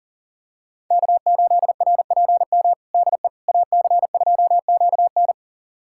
Morse code, heard as a keyed tone, about 40 wpm, 700 Hz. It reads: K8RPM DE AC2QN
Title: USA Callsign DE Callsign - 5 Characters - 40wpm